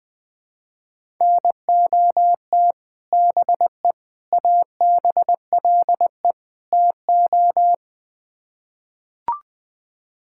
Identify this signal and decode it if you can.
Morse code, heard as a keyed tone, about 20 wpm, 700 Hz. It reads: NOT BE ABLE TO
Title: Build A Phrase - 4-Word Phrases - 20wpm